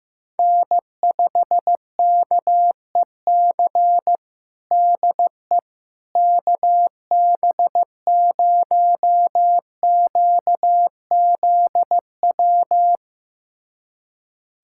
Morse code, heard as a keyed tone, about 15 wpm, 700 Hz. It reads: N5KEC DE KB0QZW
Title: USA Callsign DE Callsign 15wpm